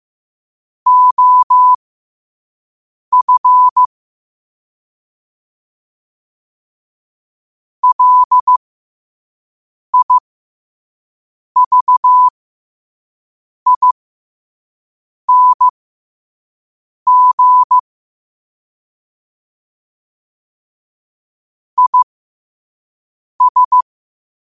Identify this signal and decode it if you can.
Morse code, heard as a keyed tone, about 15 wpm, spaced as for 5 wpm, 1000 Hz. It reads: OF LIVING IS
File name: qotdfiles / 5wpm